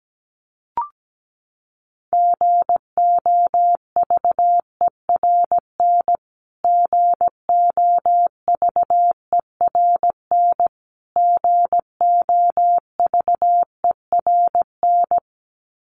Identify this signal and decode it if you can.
Morse code, heard as a keyed tone, about 17 wpm, 700 Hz. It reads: GOVERN GOVERN GOVERN
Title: Top 500 Words Repeated 17wpm